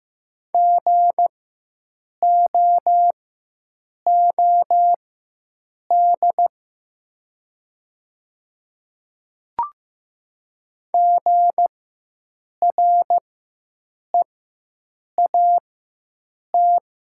Morse code, heard as a keyed tone, about 15 wpm, spaced as for 7 wpm, 700 Hz. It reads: GOOD GREAT
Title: 4x Character Spacing - Top 300 Words - 15wpm